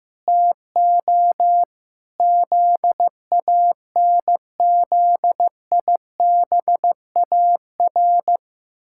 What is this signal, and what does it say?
Morse code, heard as a keyed tone, about 16 wpm, 700 Hz. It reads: TO ZANZIBAR
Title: Pangram Challenge v2 - 15wpm